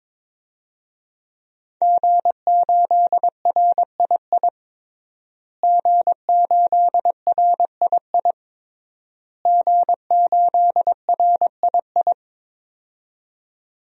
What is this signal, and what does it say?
Morse code, heard as a keyed tone, about 22 wpm, 700 Hz. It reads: G8RII G8RII G8RII
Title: DXCC-100 Call Signs - 5 Characters - Repeated 3x - 22wpm